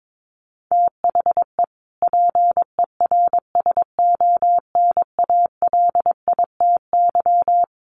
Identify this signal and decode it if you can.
Morse code, heard as a keyed tone, about 22 wpm, 700 Hz. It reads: THE PERSONALITY
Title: Sentences from Top 1400 Words 22wpm